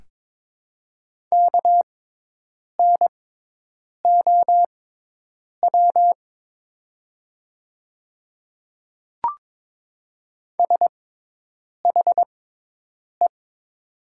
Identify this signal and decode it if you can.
Morse code, heard as a keyed tone, about 22 wpm, spaced as for 8 wpm, 700 Hz. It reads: KNOW SHE
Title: 6x Character Spacing - Top 100 Words - 22wpm